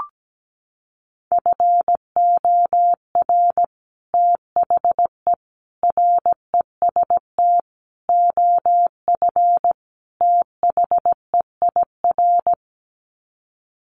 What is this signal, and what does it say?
Morse code, heard as a keyed tone, about 17 wpm, 700 Hz. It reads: FOR THE REST OF THEIR
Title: Sets of 5 Words Encore2 17wpm